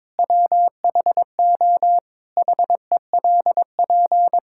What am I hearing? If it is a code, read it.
Morse code, heard as a keyed tone, about 22 wpm, 700 Hz. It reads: WHO HELP
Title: Sentences from Top 1000 Words - Review Part 5 - 22wpm